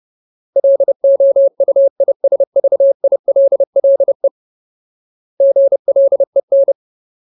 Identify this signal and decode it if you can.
Morse code, heard as a keyed tone, about 30 wpm, 550 Hz. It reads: LOUISVILLE GLEN
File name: Ultimate Word List / 30WPM Ultimate